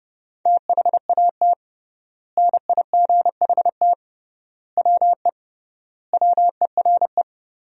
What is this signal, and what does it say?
Morse code, heard as a keyed tone, about 31 wpm, 700 Hz. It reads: THAT NIGHT WE WERE
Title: 3x Word-Spacing - Sentences from Top 300 Words - 30wpm